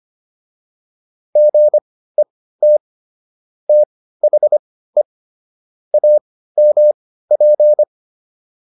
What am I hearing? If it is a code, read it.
Morse code, heard as a keyed tone, about 25 wpm, spaced as for 15 wpm, 600 Hz. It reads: GET THE AMP